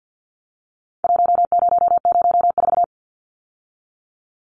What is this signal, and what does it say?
Morse code, heard as a keyed tone, about 48 wpm, 700 Hz. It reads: 1004
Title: Numbers - 4-Digit 50wpm